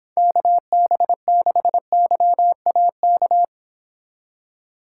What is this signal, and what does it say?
Morse code, heard as a keyed tone, about 29 wpm, 700 Hz. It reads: KB6YAK